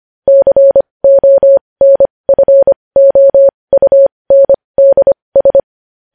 Morse code, heard as a keyed tone, about 25 wpm, spaced as for 20 wpm, 550 Hz. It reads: CONFOUNDS
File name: podcast / News headlines-1x-25x20